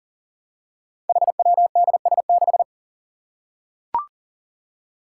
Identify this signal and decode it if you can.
Morse code, heard as a keyed tone, about 39 wpm, 700 Hz. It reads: HWDS6
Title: Five Challenging Letter-Number Combinations 40wpm